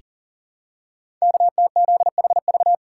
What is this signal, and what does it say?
Morse code, heard as a keyed tone, about 39 wpm, 700 Hz. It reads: KTZHV
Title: Code Groups - 5 Letters - 40wpm